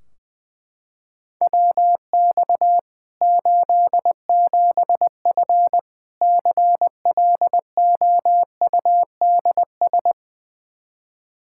Morse code, heard as a keyed tone, about 21 wpm, 700 Hz. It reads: WX 87F CLOUDS
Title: K1USN SST QSO Practice 20wpm